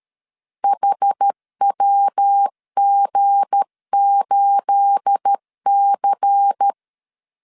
Morse code, heard as a keyed tone, about 13 wpm, 800 Hz. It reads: HWG8C
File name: LEZIONI / 9 28 12wpm 2351